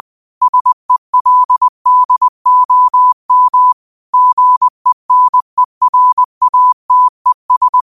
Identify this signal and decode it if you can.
Morse code, heard as a keyed tone, about 20 wpm, 1000 Hz. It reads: SELDOM GENERATES
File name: qotdfiles / 20wpm